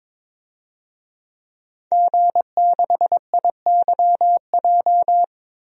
Morse code, heard as a keyed tone, about 22 wpm, 700 Hz. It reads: G6IYJ
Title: DXCC-100 Call Signs - 5 Characters - Repeated 3x - 22wpm